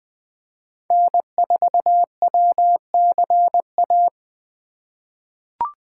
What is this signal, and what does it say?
Morse code, heard as a keyed tone, about 20 wpm, 700 Hz. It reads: N4WCA
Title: USA Call Signs - 3x Character Spacing - 20wpm